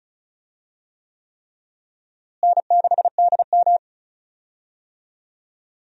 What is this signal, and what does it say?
Morse code, heard as a keyed tone, about 36 wpm, 700 Hz. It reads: N6DM
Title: USA Call Signs - 4 Characters - Repeated 3x - 35wpm